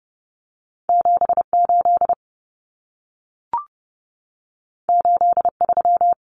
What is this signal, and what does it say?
Morse code, heard as a keyed tone, about 30 wpm, 700 Hz. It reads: 78 83
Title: Numbers - 2-Digit 30wpm